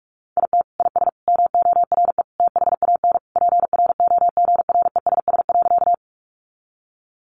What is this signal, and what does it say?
Morse code, heard as a keyed tone, about 44 wpm, 700 Hz. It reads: IT IS MORE THAN PROGRESS.
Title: Sentences from Top 1400 Words 45wpm